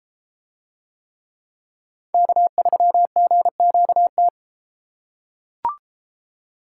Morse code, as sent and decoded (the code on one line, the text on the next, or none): -.- ...-- --. --.- -
K3GQT